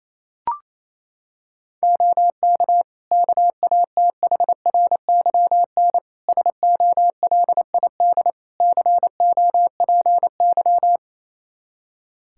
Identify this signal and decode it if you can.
Morse code, heard as a keyed tone, about 28 wpm, 700 Hz. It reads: OK KATHRYN SOLID COPY
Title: K1USN SST QSO Practice 28wpm